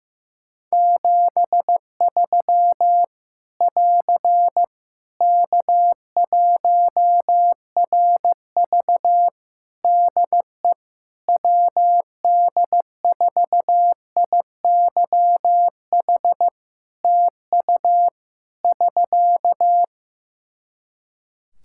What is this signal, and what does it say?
Morse code, heard as a keyed tone, about 15 wpm, 700 Hz. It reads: 73 <AR> K1RV DE WD4IYH TU <SK>
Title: Practice QSO - Exchange 4 - Ending - 15wpm